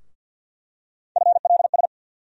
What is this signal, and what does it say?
Morse code, heard as a keyed tone, about 53 wpm, 700 Hz. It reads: FLS
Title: USA Call Sign Postfixes - 3 Letters 50wpm